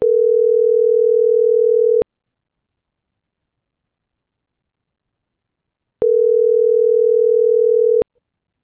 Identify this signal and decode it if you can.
Ringback tone (North America), 440 + 480 Hz.
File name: CENTRAL GENERAL / out yarit.rdz